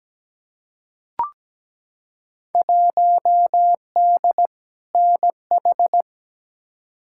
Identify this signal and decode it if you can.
Morse code, heard as a keyed tone, about 17 wpm, 700 Hz. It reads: 1D NH
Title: Field Day Exchanges - 17wpm